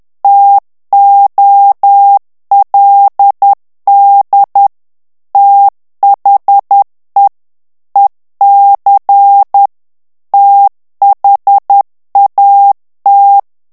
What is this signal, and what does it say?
Morse code, heard as a keyed tone, about 10 wpm, 800 Hz. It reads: TOLD THE EC THAT
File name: ANGLAIS texte clair 3 vitesse 10 mots minute